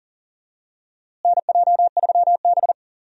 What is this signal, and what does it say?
Morse code, heard as a keyed tone, about 40 wpm, 700 Hz. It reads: NJ3B
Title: USA Call Signs - 4 Characters - Repeated 3x - 40wpm